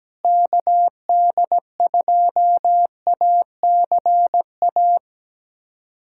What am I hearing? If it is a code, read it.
Morse code, heard as a keyed tone, about 17 wpm, 700 Hz. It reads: KD2ACA
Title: USA Callsign DE Callsign 17wpm